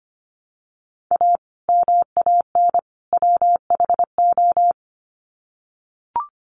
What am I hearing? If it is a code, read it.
Morse code, heard as a keyed tone, about 25 wpm, 700 Hz. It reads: A MAN WHO
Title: Sets of 3 Words 25wpm